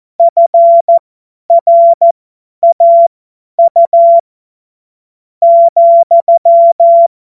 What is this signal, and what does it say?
Morse code, heard as a keyed tone, about 14 wpm, spaced as for 8 wpm, 680 Hz. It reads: FRAU,
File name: Morsekurs / Witze / Witz 06. September 2022 50 F70